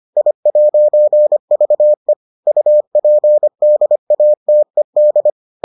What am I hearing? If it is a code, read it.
Morse code, heard as a keyed tone, about 25 wpm, 600 Hz. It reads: I'VE UPDATED